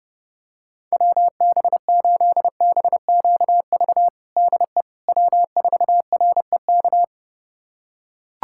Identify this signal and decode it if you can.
Morse code, heard as a keyed tone, about 30 wpm, 700 Hz. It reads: WB8BQV DE W4REK
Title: USA Callsign DE Callsign 30wpm